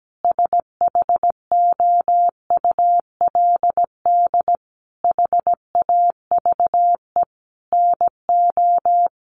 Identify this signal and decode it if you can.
Morse code, heard as a keyed tone, about 17 wpm, 700 Hz. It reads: SHOULD HAVE NO
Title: Sentences from Top 1500 Words 17wpm